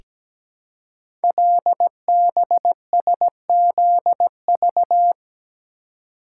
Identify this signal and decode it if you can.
Morse code, heard as a keyed tone, about 17 wpm, 700 Hz. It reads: LBSZV
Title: Code Groups - 5 Letters - 17wpm